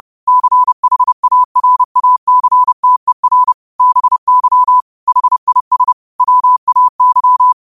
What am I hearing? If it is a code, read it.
Morse code, heard as a keyed tone, about 30 wpm, 1000 Hz. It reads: CHARACTER BY HIS WAY